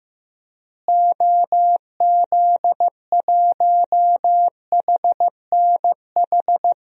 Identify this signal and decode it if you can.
Morse code, heard as a keyed tone, about 16 wpm, 700 Hz. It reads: OZ1HNH
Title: DXCC-100 Call Signs - 6 Characters - Repeated 3x - 15wpm